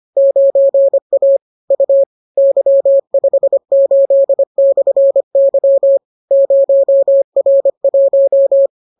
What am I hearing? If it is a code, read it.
Morse code, heard as a keyed tone, about 25 wpm, 550 Hz. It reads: 9A U Y58/Y 0R1